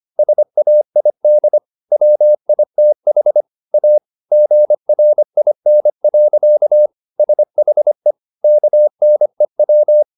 Morse code, heard as a keyed tone, about 25 wpm, 600 Hz. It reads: SAID WITH A GRIN. SHE KNEW